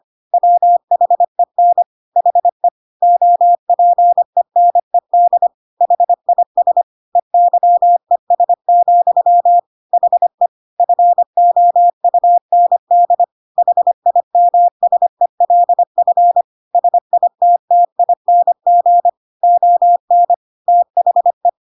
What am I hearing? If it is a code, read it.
Morse code, heard as a keyed tone, about 25 wpm, 700 Hz. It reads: WHEN HE OPENED HIS EYES, HE FOUND HIMSELF SITTING ON THE
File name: Pinocchio / Pinocchio ch1 25 25 ik1hge